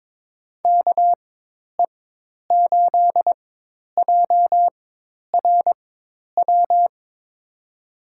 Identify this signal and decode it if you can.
Morse code, heard as a keyed tone, about 22 wpm, spaced as for 10 wpm, 700 Hz. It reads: KE8JRW
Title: USA Call Signs - 6 Characters - 4x Character Spacing - 22wpm